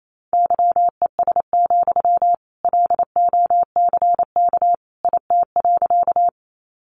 Morse code, heard as a keyed tone, about 28 wpm, 700 Hz. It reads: YES, LOCK IT.
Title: Sentences from Top 1300 Words 28wpm